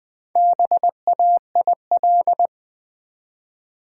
Morse code, heard as a keyed tone, about 20 wpm, 700 Hz. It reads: BAIL
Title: Build A Word - 1 Letter At A Time - 3 to 5-Letter Words - 20wpm